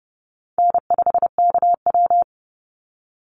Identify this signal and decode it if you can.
Morse code, heard as a keyed tone, about 30 wpm, 700 Hz. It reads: N5KW
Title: Easy Call Signs Repeated 3x - 30wpm